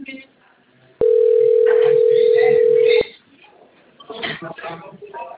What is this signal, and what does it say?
Ringback tone (North America), 440 + 480 Hz, with speech.
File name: rg-1600-unknown- - - .73289